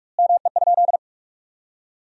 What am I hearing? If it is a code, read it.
Morse code, heard as a keyed tone, about 46 wpm, 700 Hz. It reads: ME?
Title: 2x Word-Spacing - Sentences from Top 200 Words - 45wpm